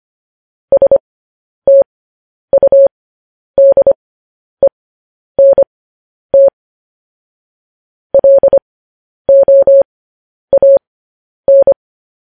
Morse code, heard as a keyed tone, about 25 wpm, spaced as for 10 wpm, 550 Hz. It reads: STUDENT LOAN
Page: news-headlines-in-morse-code-at-10-wpm